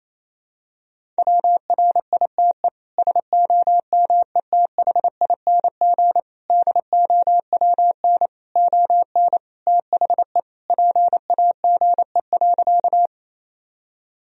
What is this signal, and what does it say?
Morse code, heard as a keyed tone, about 28 wpm, 700 Hz. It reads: WRITE SOMETHING DOWN ON THE PAGE.
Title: Sentences from Top 1000 Words - Review Part 1 - 28wpm